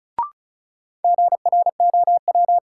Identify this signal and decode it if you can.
Morse code, heard as a keyed tone, about 36 wpm, 700 Hz. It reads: GROW
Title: Sentences from Top 300 Words - New Word Drill - 35wpm